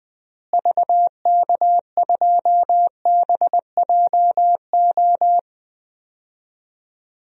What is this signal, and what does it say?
Morse code, heard as a keyed tone, about 20 wpm, 700 Hz. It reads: VK2BJO
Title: DXCC-100 Call Signs - 6 Characters - Repeated 2x - 20wpm